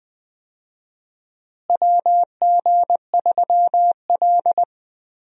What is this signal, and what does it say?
Morse code, heard as a keyed tone, about 20 wpm, 700 Hz. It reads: WG3L
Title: USA Call Signs - 4 Characters - Repeated 3x - 20wpm